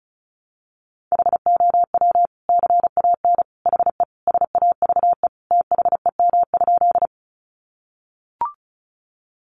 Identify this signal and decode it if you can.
Morse code, heard as a keyed tone, about 34 wpm, 700 Hz. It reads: HOW CAN HE SAVE THEM?
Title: Sentences from Top 1100 Words 35wpm